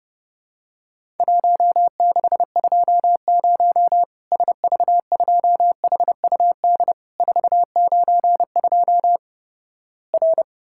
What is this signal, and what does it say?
Morse code, heard as a keyed tone, about 32 wpm, 700 Hz. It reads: 1620 SV2HUD 492 R
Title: Worked All Europe DX Contest - QTC Exchange - 30wpm